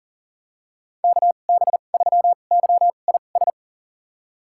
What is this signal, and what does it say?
Morse code, heard as a keyed tone, about 40 wpm, spaced as for 29 wpm, 700 Hz. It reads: KB3YIS
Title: USA Call Signs - 6 Characters - 2x Character Spacing - 40wpm